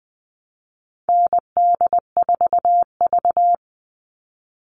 Morse code, heard as a keyed tone, about 20 wpm, 700 Hz. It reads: ND4V
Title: All Cut Numbers - 4-Digit 20wpm